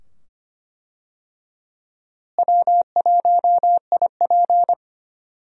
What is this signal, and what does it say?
Morse code, heard as a keyed tone, about 25 wpm, 700 Hz. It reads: W1IP
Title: USA Call Signs - 4 Characters - Repeated 2x - 25wpm